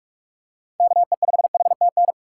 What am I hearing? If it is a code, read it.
Morse code, heard as a keyed tone, about 43 wpm, 700 Hz. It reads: KE5HTN
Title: USA Call Signs - 6 Characters - 3x Character Spacing - 45wpm